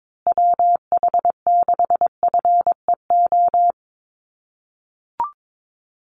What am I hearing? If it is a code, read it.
Morse code, heard as a keyed tone, about 22 wpm, 700 Hz. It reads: WH6FEO
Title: USA Call Signs Repeated 3x 22wpm